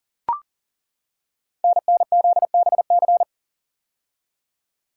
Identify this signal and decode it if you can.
Morse code, heard as a keyed tone, about 39 wpm, 700 Hz. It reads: NNZBC
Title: Code Groups - 5 Letters - 40wpm